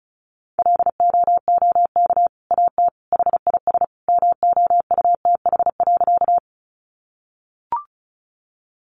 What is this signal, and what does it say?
Morse code, heard as a keyed tone, about 34 wpm, 700 Hz. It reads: LOOK AT HIS MOUTH.
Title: Sentences from Top 1500 Words 35wpm